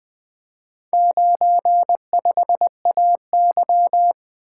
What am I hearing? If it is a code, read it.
Morse code, heard as a keyed tone, about 20 wpm, 700 Hz. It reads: 95AY
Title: Four Letter-Number Combinations - 20wpm